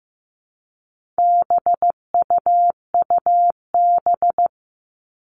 Morse code, heard as a keyed tone, about 15 wpm, 700 Hz. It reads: BUUB
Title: All Cut Numbers - 4-Digit 15wpm